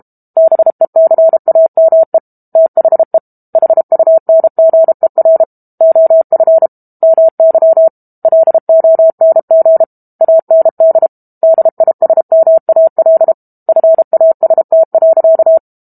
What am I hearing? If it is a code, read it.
Morse code, heard as a keyed tone, about 32 wpm, 650 Hz. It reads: BECAME THE HUNGER OF MY LONG AND DISMAL FAST.
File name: Book Two / 06-Chapter 6 - 32 WPM